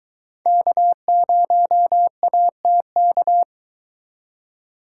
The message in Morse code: K0ATK

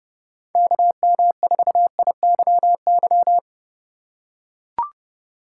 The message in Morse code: KM4IYY

